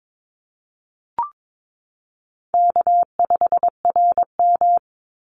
Morse code, heard as a keyed tone, about 22 wpm, 700 Hz. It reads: K5RM
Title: Easy Call Signs - 22wpm